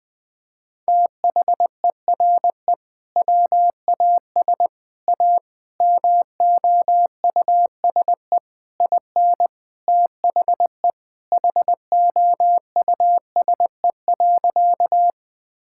Morse code, heard as a keyed tone, about 20 wpm, 700 Hz. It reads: THERE WAS A MOUSE IN THE HOUSE.
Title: Sentences from Top 1000 Words - Review Part 5 - 20wpm